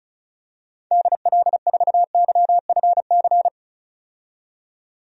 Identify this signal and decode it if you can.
Morse code, heard as a keyed tone, about 36 wpm, 700 Hz. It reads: DL4YFC
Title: DXCC-100 Call Signs - 6 Characters - 35wpm